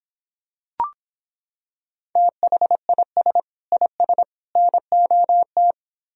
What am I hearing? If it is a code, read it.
Morse code, heard as a keyed tone, about 26 wpm, 700 Hz. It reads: THIS IS NOT